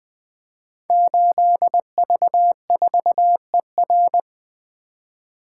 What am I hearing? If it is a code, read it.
Morse code, heard as a keyed tone, about 20 wpm, 700 Hz. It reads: 8V4ER
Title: Code Groups - 5 Letter:Numbers - 20wpm